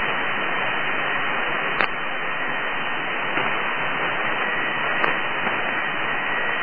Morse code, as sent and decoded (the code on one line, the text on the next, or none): none